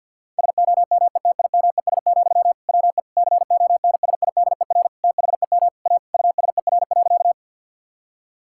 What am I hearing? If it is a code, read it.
Morse code, heard as a keyed tone, about 52 wpm, 700 Hz. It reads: SOMETIMES, WE CONSIDER THEM A USER.
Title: Sentences from Top 1000 Words - Review Part 5 - 50wpm